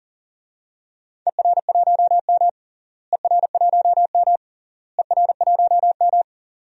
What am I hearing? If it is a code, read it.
Morse code, heard as a keyed tone, about 40 wpm, 700 Hz. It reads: ER1M ER1M ER1M
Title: DXCC-100 Call Signs - 4 Characters - Repeated 3x - 40wpm